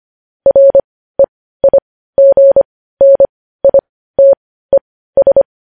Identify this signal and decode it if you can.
Morse code, heard as a keyed tone, about 25 wpm, spaced as for 15 wpm, 550 Hz. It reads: REIGNITES